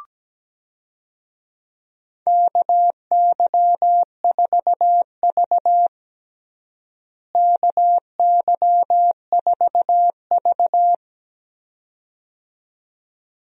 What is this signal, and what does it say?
Morse code, heard as a keyed tone, about 17 wpm, 700 Hz. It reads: KY4V KY4V
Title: USA Call Signs - 4 Characters - Repeated 2x - 17wpm